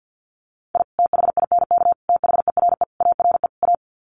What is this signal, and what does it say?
Morse code, heard as a keyed tone, about 48 wpm, 700 Hz. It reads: I THINK THERE ARE A
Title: Sets of 5 Words Encore2 50wpm